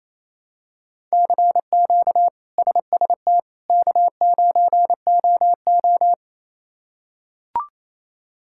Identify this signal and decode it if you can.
Morse code, heard as a keyed tone, about 28 wpm, 700 Hz. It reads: CQ SST K9OO
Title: K1USN SST QSO Practice - 28wpm